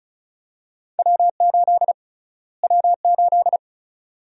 Morse code, heard as a keyed tone, about 36 wpm, 700 Hz. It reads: W8 W8
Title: USA Call Sign Prefixes - Repeated 2x - 35wpm